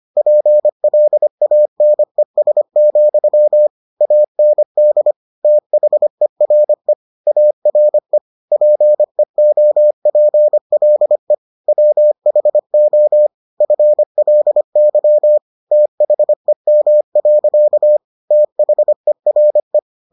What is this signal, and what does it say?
Morse code, heard as a keyed tone, about 25 wpm, 600 Hz. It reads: PLANES, AND THERE ARE PEOPLE WHO FLY THEM. THERE